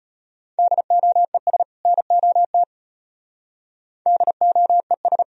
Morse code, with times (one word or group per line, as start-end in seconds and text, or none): 0.6-1.6 DOES
1.9-2.6 NOT
4.1-5.2 DOES